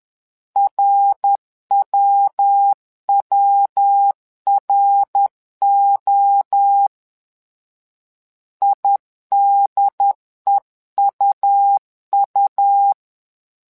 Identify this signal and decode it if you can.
Morse code, heard as a keyed tone, about 10 wpm, 800 Hz. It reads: RWWRO IDEUU
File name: LEZIONI / 3 MWRO / 3 MWRO 10wpm 2148